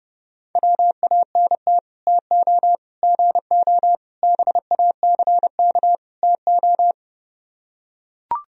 Morse code, heard as a keyed tone, about 32 wpm, 700 Hz. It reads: WANT TO GO BACK TO